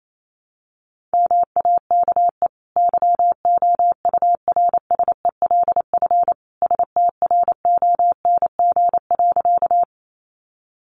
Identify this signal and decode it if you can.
Morse code, heard as a keyed tone, about 28 wpm, 700 Hz. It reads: MAKE YOURSELF STRONG.
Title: Sentences from Top 600 Words 28wpm